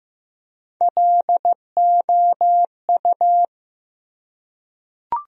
Morse code, heard as a keyed tone, about 15 wpm, 700 Hz. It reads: LOU
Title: USA Common Names - Max 4 Letters - 15wpm